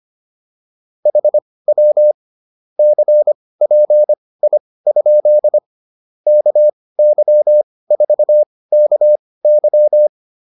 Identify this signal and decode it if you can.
Morse code, heard as a keyed tone, about 25 wpm, spaced as for 18 wpm, 600 Hz. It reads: HW CPI? KY4KY